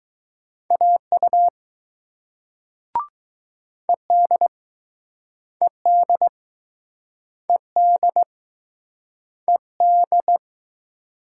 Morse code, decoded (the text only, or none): AU ED ED ED ED